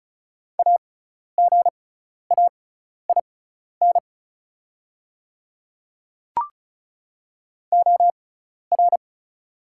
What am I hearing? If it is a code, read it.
Morse code, heard as a keyed tone, about 34 wpm, spaced as for 12 wpm, 700 Hz. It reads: AGAIN OR
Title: 6x Character Spacing - Top 300 Words - 35wpm